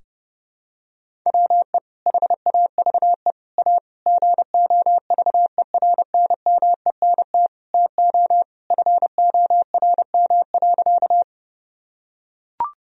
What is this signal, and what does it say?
Morse code, heard as a keyed tone, about 32 wpm, 700 Hz. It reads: WE HAVE A GOVERNMENT TO FORM.
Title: Sentences from Top 1000 Words - Review Part 1 - 30wpm